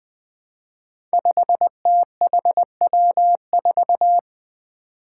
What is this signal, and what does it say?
Morse code, heard as a keyed tone, about 20 wpm, 700 Hz. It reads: 5THW4
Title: Five Challenging Letter-Number Combinations 20wpm